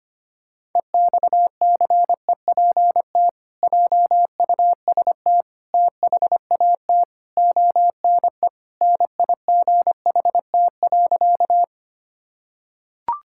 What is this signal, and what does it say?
Morse code, heard as a keyed tone, about 25 wpm, 700 Hz. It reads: EXCEPT JUST THAT ONE NIGHT.
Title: Sentences from Top 1600 Words 25wpm